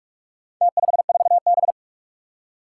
Morse code, heard as a keyed tone, about 46 wpm, 700 Hz. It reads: T54B
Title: Four Challenging Letter-Number Combinations 45wpm